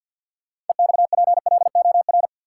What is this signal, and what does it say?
Morse code, heard as a keyed tone, about 53 wpm, 700 Hz. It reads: EXPLOR
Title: Sentences from Top 1100 Words - New Word Drill - 50wpm